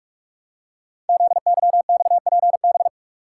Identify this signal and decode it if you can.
Morse code, heard as a keyed tone, about 44 wpm, 700 Hz. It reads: ZYXPB
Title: Code Groups - 5 Letter:Numbers - 45wpm